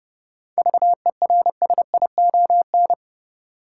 Morse code, heard as a keyed tone, about 30 wpm, 700 Hz. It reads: VERSION
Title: Sentences from Top 1400 Words - New Word Drill - 30wpm